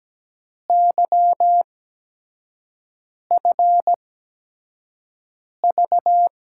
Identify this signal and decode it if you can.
Morse code, heard as a keyed tone, about 17 wpm, spaced as for 5 wpm, 700 Hz. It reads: YFV
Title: USA Call Signs - 6 Characters - 8x Character Spacing - 17wpm